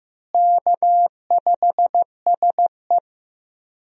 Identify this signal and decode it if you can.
Morse code, heard as a keyed tone, about 15 wpm, 700 Hz. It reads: K5SE
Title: USA Call Signs - 4 Characters - Repeated 3x - 15wpm